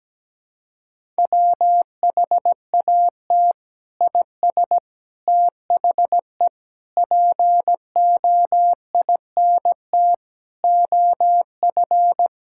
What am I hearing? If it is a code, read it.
Morse code, heard as a keyed tone, about 17 wpm, 700 Hz. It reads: WHAT IS THE POINT OF
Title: Sentences from Top 1000 Words - Review Part 5 - 17wpm